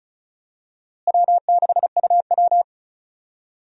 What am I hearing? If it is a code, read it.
Morse code, heard as a keyed tone, about 36 wpm, 700 Hz. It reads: W6UW
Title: USA Call Signs - 4 Characters - Repeated 3x - 35wpm